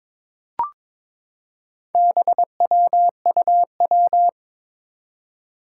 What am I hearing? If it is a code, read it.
Morse code, heard as a keyed tone, about 21 wpm, 700 Hz. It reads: BWUW
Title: Four Challenging Letter-Number Combinations 22wpm